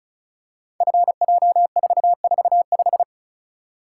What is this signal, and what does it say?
Morse code, heard as a keyed tone, about 36 wpm, 690 Hz. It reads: FJ445
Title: Code Groups - 5 Letter:Numbers - 35wpm